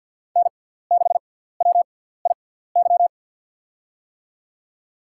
Morse code, heard as a keyed tone, about 47 wpm, spaced as for 17 wpm, 700 Hz. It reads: N6WIY
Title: USA Call Signs - 5 Characters - 6x Character Spacing - 50wpm